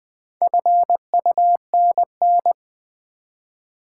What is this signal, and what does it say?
Morse code, heard as a keyed tone, about 20 wpm, 700 Hz. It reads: FUNN
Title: Build A Word - 1 Letter At A Time - 3 to 5-Letter Words - 20wpm